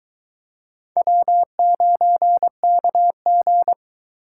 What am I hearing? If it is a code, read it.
Morse code, heard as a keyed tone, about 23 wpm, 700 Hz. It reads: W9KG